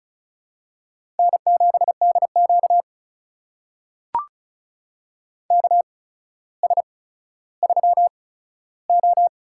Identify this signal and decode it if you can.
Morse code, heard as a keyed tone, about 36 wpm, spaced as for 15 wpm, 700 Hz. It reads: N7DQ KS3O
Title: USA Call Signs - 4 Characters - 8x Character Spacing - 35wpm